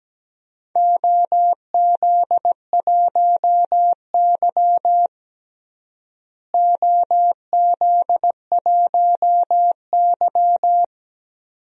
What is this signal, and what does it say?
Morse code, heard as a keyed tone, about 17 wpm, 700 Hz. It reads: OZ1Y OZ1Y
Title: DXCC-100 Call Signs - 4 Characters - Repeated 2x - 17wpm